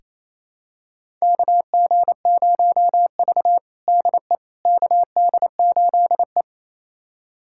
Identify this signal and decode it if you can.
Morse code, heard as a keyed tone, about 28 wpm, 700 Hz. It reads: KG0V DE KD8E
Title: USA Callsign DE Callsign - 4 Characters - 28wpm